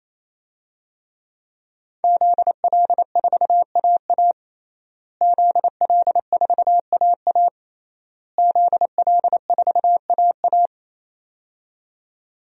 Morse code, heard as a keyed tone, about 28 wpm, 700 Hz. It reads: ZL4AA ZL4AA ZL4AA
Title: DXCC-100 Call Signs - 5 Characters - Repeated 3x - 28wpm